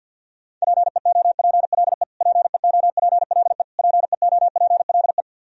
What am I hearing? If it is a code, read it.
Morse code, heard as a keyed tone, about 48 wpm, 700 Hz. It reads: PEOPLE PEOPLE PEOPLE
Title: Top 100 Words Repeated 50wpm